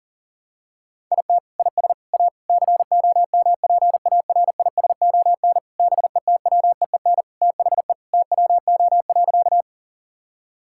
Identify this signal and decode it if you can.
Morse code, heard as a keyed tone, about 40 wpm, 700 Hz. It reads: IT IS A COMPARISON BETWEEN THE TWO.
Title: Sentences from Top 1600 Words 40wpm